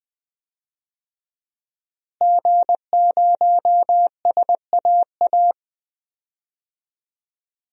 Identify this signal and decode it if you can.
Morse code, heard as a keyed tone, about 20 wpm, 700 Hz. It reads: G0SAA
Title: DXCC-100 Call Signs - 5 Characters - Repeated 2x - 20wpm